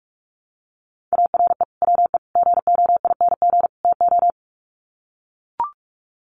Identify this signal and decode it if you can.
Morse code, heard as a keyed tone, about 44 wpm, 700 Hz. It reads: ARE WE GOING TO
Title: Sets of 4 Words 45wpm